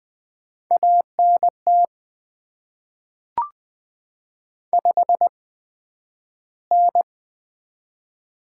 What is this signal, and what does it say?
Morse code, heard as a keyed tone, about 20 wpm, 700 Hz. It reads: A N T 5 N
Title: Top 20 QSO Elements - 8x Character Spacing - 20wpm